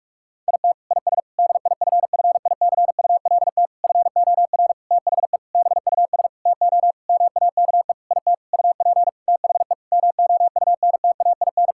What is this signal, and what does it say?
Morse code, heard as a keyed tone, about 41 wpm, 700 Hz. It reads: IT IS DIFFICULT FOR THE BUS TO MAKE IT UP THE MOUNTAIN
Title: Sentences from Top 1000 Words - Review Part 6 - 45wpm